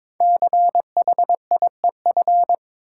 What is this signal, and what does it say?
Morse code, heard as a keyed tone, about 22 wpm, 700 Hz. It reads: CHIEF
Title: Sets of 5 Words Encore - New Word Drill - 22wpm